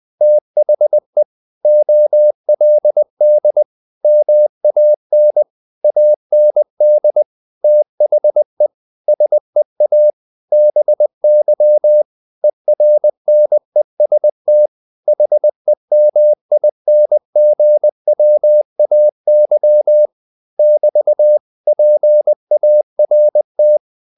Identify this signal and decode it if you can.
Morse code, heard as a keyed tone, about 20 wpm, 600 Hz. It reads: THE OLD MAN AND THE SEA BY ERNEST HEMINGWAY <BT> PART
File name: the old man and the sea / the old man and the sea part 1 20wpm